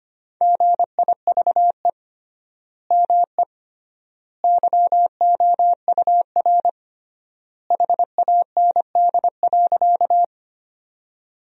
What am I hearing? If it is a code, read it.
Morse code, heard as a keyed tone, about 25 wpm, 700 Hz. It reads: GIVE ME YOUR HAND.
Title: 3x Word-Spacing - Sentences from Top 200 Words - 25wpm